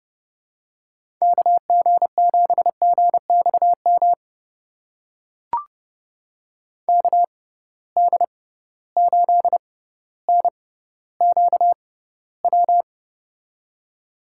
Morse code, heard as a keyed tone, about 32 wpm, spaced as for 16 wpm, 700 Hz. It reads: KG7GXM KD8NQW
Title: USA Call Signs - 6 Characters - 6x Character Spacing - 30wpm